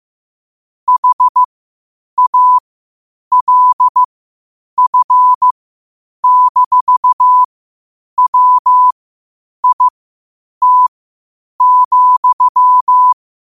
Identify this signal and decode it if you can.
Morse code, heard as a keyed tone, about 15 wpm, spaced as for 8 wpm, 1000 Hz. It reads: HALF-WIT,
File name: qotdfiles / 8wpm